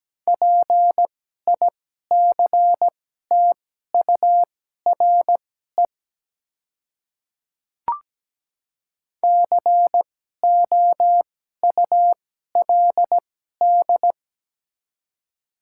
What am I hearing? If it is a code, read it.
Morse code, heard as a keyed tone, about 17 wpm, spaced as for 12 wpm, 700 Hz. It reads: PICTURE COULD
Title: 2x Character Spacing - Top 200 Words - 17wpm